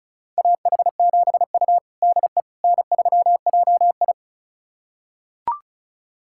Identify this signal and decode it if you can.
Morse code, heard as a keyed tone, about 36 wpm, 690 Hz. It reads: AH7U DE N3JI
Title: USA Callsign DE Callsign - 4 Characters - 35wpm